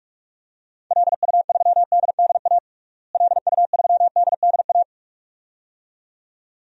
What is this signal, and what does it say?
Morse code, heard as a keyed tone, about 43 wpm, 700 Hz. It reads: LU3DDA LU3DDA
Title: DXCC-100 Call Signs - 6 Characters - Repeated 2x - 45wpm